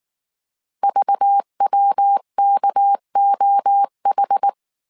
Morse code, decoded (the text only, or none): VWKOH